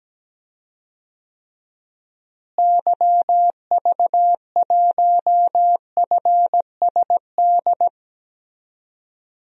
Morse code, heard as a keyed tone, about 17 wpm, 700 Hz. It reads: YV1FSD
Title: DXCC-100 Call Signs - 6 Characters - Repeated 2x - 17wpm